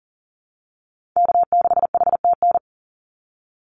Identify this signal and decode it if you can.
Morse code, heard as a keyed tone, about 37 wpm, 700 Hz. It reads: K6HTN
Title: Call Signs - 40wpm